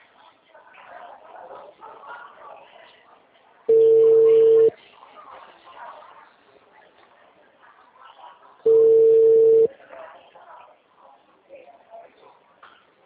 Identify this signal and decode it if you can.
Ringback tone (China, 450 Hz), with speech.
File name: out- -2012